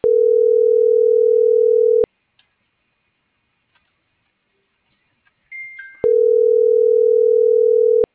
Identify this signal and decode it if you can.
Ringback tone (North America), 440 + 480 Hz.